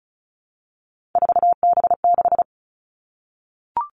Morse code, heard as a keyed tone, about 36 wpm, 700 Hz. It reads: 4B6
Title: Three Challenging Letter-Number Combinations 35wpm